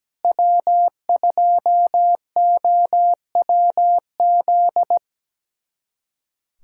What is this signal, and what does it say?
Morse code, heard as a keyed tone, about 17 wpm, 700 Hz. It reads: W2OWZ
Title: USA Callsign DE Callsign 17wpm